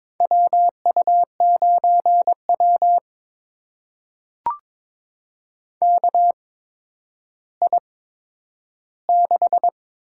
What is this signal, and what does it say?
Morse code, heard as a keyed tone, about 22 wpm, spaced as for 11 wpm, 700 Hz. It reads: WU9W KI6
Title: USA Call Signs - 8x Character Spacing - 22wpm